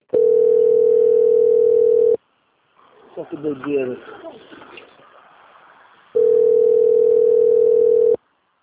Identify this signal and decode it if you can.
Ringback tone (North America), 440 + 480 Hz, with speech.
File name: out- - - - - .8896